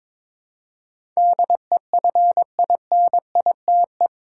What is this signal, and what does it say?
Morse code, heard as a keyed tone, about 22 wpm, 700 Hz. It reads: DEFINITE
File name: practice / 22wpm / Eight Letter Words - 22wpm